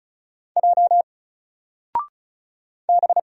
Morse code, heard as a keyed tone, about 36 wpm, 700 Hz. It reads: J B
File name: Single-Letter-Number-35wpm